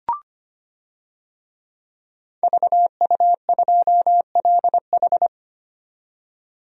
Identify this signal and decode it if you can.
Morse code, heard as a keyed tone, about 25 wpm, 700 Hz. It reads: VU2LH
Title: DXCC-100 Call Signs - 5 Characters - 25wpm